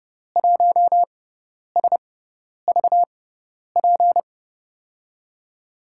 Morse code, heard as a keyed tone, about 30 wpm, spaced as for 10 wpm, 700 Hz. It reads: 1SVP
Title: USA Call Signs - 6x Character Spacing - 30wpm